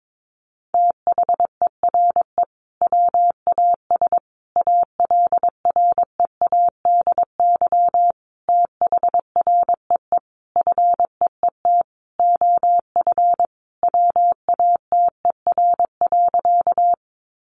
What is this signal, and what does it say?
Morse code, heard as a keyed tone, about 22 wpm, 700 Hz. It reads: THERE WAS ALREADY THREE FEET OF WATER.
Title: Sentences from Top 700 Words 22wpm